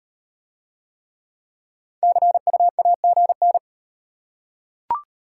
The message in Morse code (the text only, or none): CUAGN